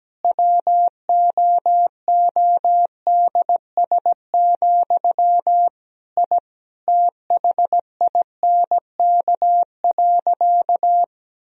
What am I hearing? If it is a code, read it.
Morse code, heard as a keyed tone, about 17 wpm, 700 Hz. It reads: WOODS, I THINK.
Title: Sentences from Top 1000 Words - Review Part 2 - 17wpm